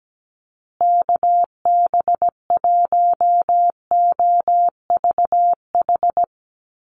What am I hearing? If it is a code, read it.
Morse code, heard as a keyed tone, about 17 wpm, 700 Hz. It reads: KB1OVH
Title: USA Call Signs Repeated 2x 17wpm